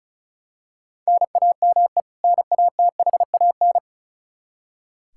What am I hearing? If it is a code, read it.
Morse code, heard as a keyed tone, about 36 wpm, 700 Hz. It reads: NAME NATHAN
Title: Practice QSO - Exchange 1 - Beginning - 35wpm